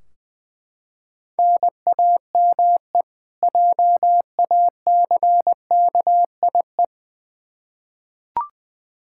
Morse code, heard as a keyed tone, about 21 wpm, 700 Hz. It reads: NAME JACKIE
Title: K1USN SST QSO Practice 20wpm